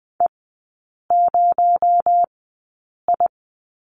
Morse code, heard as a keyed tone, about 20 wpm, spaced as for 8 wpm, 700 Hz. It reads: E0I
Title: 218 Lesson 35e - Number 0 - Three Characters 20wpm